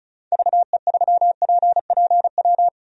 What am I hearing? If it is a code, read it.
Morse code, heard as a keyed tone, about 36 wpm, 700 Hz. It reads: VE3PPW
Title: DXCC-100 Call Signs - 6 Characters - 35wpm